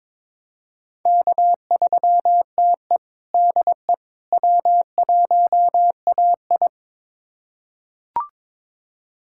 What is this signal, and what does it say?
Morse code, heard as a keyed tone, about 22 wpm, 700 Hz. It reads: K3TE DE W1AI
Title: USA Callsign DE Callsign - 4 Characters - 22wpm